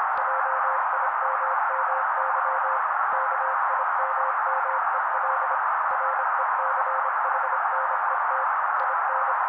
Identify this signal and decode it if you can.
Morse code, heard as a keyed tone, about 26 wpm, 530 Hz. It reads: JIMMY KIMMEL RECHNET IN
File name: daily / 25wpm noise